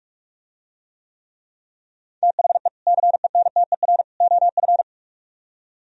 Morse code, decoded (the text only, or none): THE CENTER OF